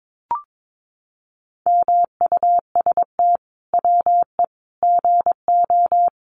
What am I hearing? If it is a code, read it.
Morse code, heard as a keyed tone, about 22 wpm, 700 Hz. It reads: MUST WE GO
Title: Sentences from Top 1400 Words 22wpm